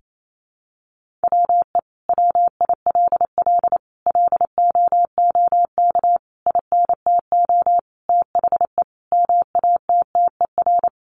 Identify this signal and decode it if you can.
Morse code, heard as a keyed tone, about 28 wpm, 700 Hz. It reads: WE WILL LOOK INTO THE MATTER
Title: Sentences from Top 1400 Words 28wpm